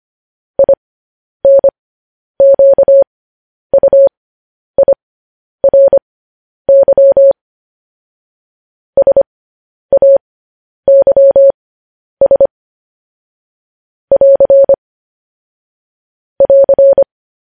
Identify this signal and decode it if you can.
Morse code, heard as a keyed tone, about 25 wpm, spaced as for 10 wpm, 550 Hz. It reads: INQUIRY SAYS <AR> <AR>
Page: News Headlines in Morse Code at 10 WPM